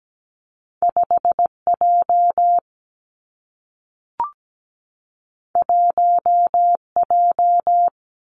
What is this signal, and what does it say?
Morse code, heard as a keyed tone, about 17 wpm, 700 Hz. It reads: 5J 1J
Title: Two Challenging Letter-Number Combinations 17wpm